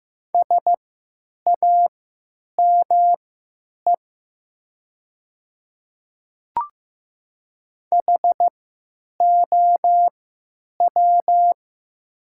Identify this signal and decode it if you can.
Morse code, heard as a keyed tone, about 15 wpm, spaced as for 9 wpm, 700 Hz. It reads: SAME HOW
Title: 3x Character Spacing - Top 500 Words - 15wpm